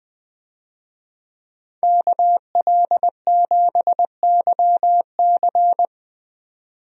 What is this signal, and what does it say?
Morse code, heard as a keyed tone, about 20 wpm, 700 Hz. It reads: KL7YC
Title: DXCC-100 Call Signs - 5 Characters - 20wpm